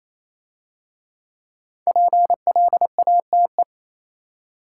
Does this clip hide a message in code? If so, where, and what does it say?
Yes: Morse code from 1.9 to 3.6 s, heard as a keyed tone, about 28 wpm, 690 Hz. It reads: PLATE